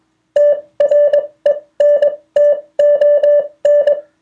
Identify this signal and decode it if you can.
Morse code, heard as a keyed tone, about 21 wpm, spaced as for 17 wpm, 580 Hz. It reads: TRENTON